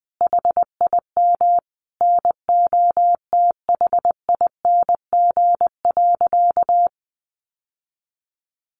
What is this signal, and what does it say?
Morse code, heard as a keyed tone, about 20 wpm, 700 Hz. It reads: HIM NOTHING.
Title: Sentences from Top 700 Words 20wpm